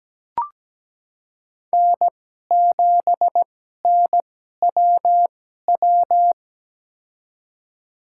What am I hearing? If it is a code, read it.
Morse code, heard as a keyed tone, about 17 wpm, spaced as for 12 wpm, 700 Hz. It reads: N7NWW
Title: USA Call Signs - 5 Characters - 2x Character Spacing - 17wpm